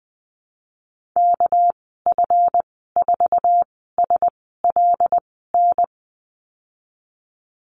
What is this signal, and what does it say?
Morse code, heard as a keyed tone, about 20 wpm, spaced as for 15 wpm, 700 Hz. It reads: KF4SLN
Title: 200 Lesson 32g - Letter k - Call Signs 20wpm